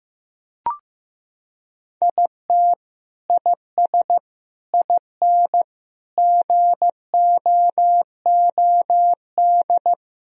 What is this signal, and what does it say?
Morse code, heard as a keyed tone, about 15 wpm, 700 Hz. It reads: IT IS IN GOOD
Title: Sentences from Top 1000 Words - Review Part 4 - 15wpm